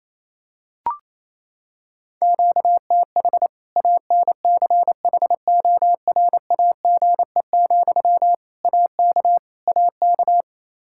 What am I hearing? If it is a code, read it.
Morse code, heard as a keyed tone, about 28 wpm, 700 Hz. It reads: QTH ANCHORAGE, AK AK
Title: K1USN SST QSO Practice 28wpm